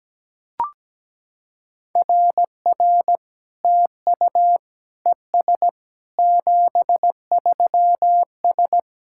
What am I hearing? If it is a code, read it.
Morse code, heard as a keyed tone, about 17 wpm, 700 Hz. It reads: RR TU ES 73S
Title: SOTA QSO Practice 17wpm